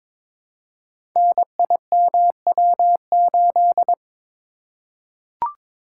Morse code, heard as a keyed tone, about 22 wpm, 700 Hz. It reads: NIMW8
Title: Code Groups - 5 Letter:Numbers - 22wpm